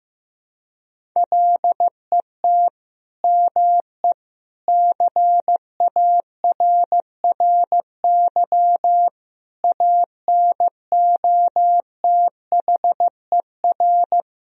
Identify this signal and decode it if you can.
Morse code, heard as a keyed tone, about 15 wpm, 700 Hz. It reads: LET ME CARRY ANOTHER
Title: Sentences from Top 1000 Words - Review Part 1 - 15wpm